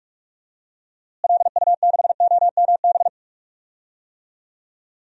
Morse code, heard as a keyed tone, about 45 wpm, 700 Hz. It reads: LU6OMB
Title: DXCC-100 Call Signs - 6 Characters - Repeated 3x - 45wpm